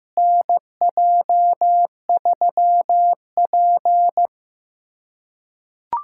Morse code, heard as a keyed tone, about 15 wpm, 700 Hz. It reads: NJ3P
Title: USA Callsign DE Callsign - 4 Characters - 15wpm